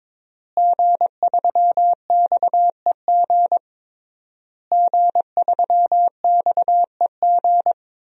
Morse code, heard as a keyed tone, about 22 wpm, 700 Hz. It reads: G3XEG G3XEG
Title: DXCC-100 Call Signs - 5 Characters - Repeated 3x - 22wpm